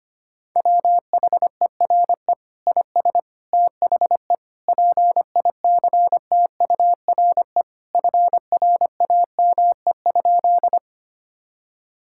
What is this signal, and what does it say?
Morse code, heard as a keyed tone, about 25 wpm, 700 Hz. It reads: WHERE IS THE PICTURE FRAME?
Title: Sentences from Top 1000 Words - Review Part 6 - 25wpm